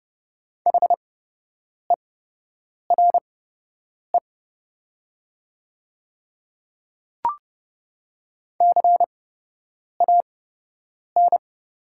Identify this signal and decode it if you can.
Morse code, heard as a keyed tone, about 30 wpm, spaced as for 8 wpm, 700 Hz. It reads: HERE CAN
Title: 8x Character Spacing - Top 200 Words - 30wpm